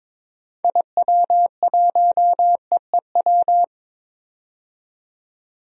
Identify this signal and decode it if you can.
Morse code, heard as a keyed tone, about 22 wpm, 700 Hz. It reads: IW1EEW
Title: DXCC-100 Call Signs - 6 Characters - 22wpm